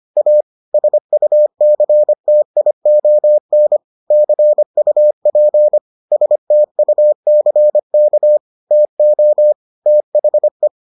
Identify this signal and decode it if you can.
Morse code, heard as a keyed tone, about 25 wpm, 600 Hz. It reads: A SUCTION CUP STUCK TO THE